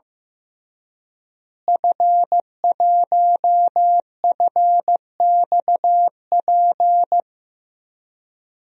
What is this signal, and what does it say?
Morse code, heard as a keyed tone, about 15 wpm, 700 Hz. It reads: F1FXP
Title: DXCC-100 Call Signs - 5 Characters - Repeated 3x - 15wpm